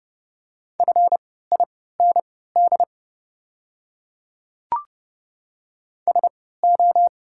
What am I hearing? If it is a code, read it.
Morse code, heard as a keyed tone, about 32 wpm, spaced as for 17 wpm, 700 Hz. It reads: FIND SO
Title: 3x Character Spacing - Top 200 Words - 30wpm